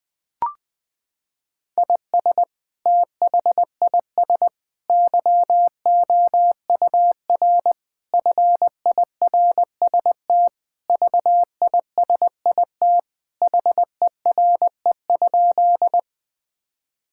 Morse code, heard as a keyed tone, about 20 wpm, 700 Hz. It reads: IS THIS YOUR FIRST VISIT HERE?
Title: Sentences from Top 1000 Words - Review Part 6 - 20wpm